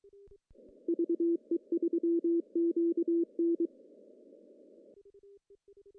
Morse code, heard as a keyed tone, about 23 wpm, 340 Hz. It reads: VE3QN
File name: CW / cqww-cw-k5tr